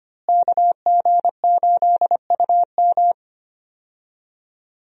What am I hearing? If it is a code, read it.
Morse code, heard as a keyed tone, about 25 wpm, 700 Hz. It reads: KG8UM
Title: USA Call Signs - 5 Characters - Repeated 2x - 25wpm